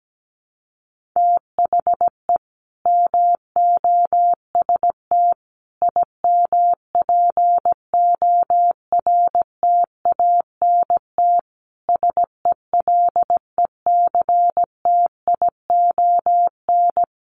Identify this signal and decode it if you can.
Morse code, heard as a keyed tone, about 17 wpm, 700 Hz. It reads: THE MOST IMPORTANT SELECTION
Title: Sentences from Top 1400 Words 17wpm